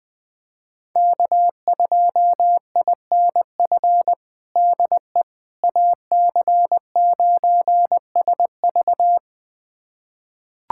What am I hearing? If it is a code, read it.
Morse code, heard as a keyed tone, about 20 wpm, 700 Hz. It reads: K2INF DE AC9SV
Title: USA Callsign DE Callsign - 5 Characters - 20wpm